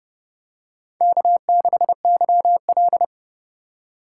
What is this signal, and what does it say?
Morse code, heard as a keyed tone, about 28 wpm, 700 Hz. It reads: K6YL